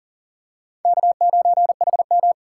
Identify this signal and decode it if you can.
Morse code, heard as a keyed tone, about 38 wpm, 700 Hz. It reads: K9HM
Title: USA Call Signs - 4 Characters - 6x Character Spacing - 40wpm